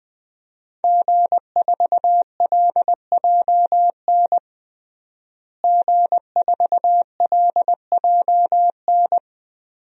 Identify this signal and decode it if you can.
Morse code, heard as a keyed tone, about 21 wpm, 700 Hz. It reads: G4LJN G4LJN
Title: DXCC-100 Call Signs - 5 Characters - Repeated 2x - 20wpm